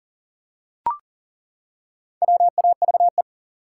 Morse code, heard as a keyed tone, about 38 wpm, 700 Hz. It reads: WAVE
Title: Four Letter Words - 40wpm